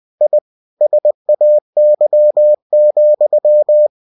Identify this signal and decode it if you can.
Morse code, heard as a keyed tone, about 20 wpm, 600 Hz. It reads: I SAY,
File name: cw-books / winnie-the-pooh / winnie-20wpm-page-0087